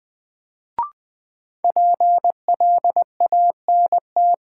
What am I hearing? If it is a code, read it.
Morse code, heard as a keyed tone, about 21 wpm, 700 Hz. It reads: PLANT
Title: Sentences from Top 1000 Words - Review Part 1 - New Word Drill - 20wpm